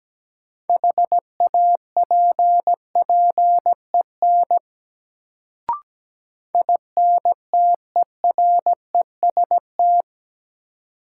Morse code, heard as a keyed tone, about 17 wpm, 700 Hz. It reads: HAPPEN INTEREST
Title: Sets of 5 Words Encore - New Word Drill - 17wpm